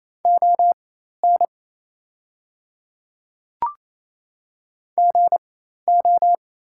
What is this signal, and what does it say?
Morse code, heard as a keyed tone, about 28 wpm, spaced as for 13 wpm, 690 Hz. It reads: ON GO
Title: 4x Character Spacing - Top 100 Words - 28wpm